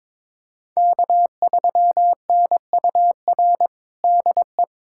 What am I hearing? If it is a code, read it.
Morse code, heard as a keyed tone, about 22 wpm, 700 Hz. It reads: K3NUR DE
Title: USA Callsign DE Callsign - 5 Characters - 22wpm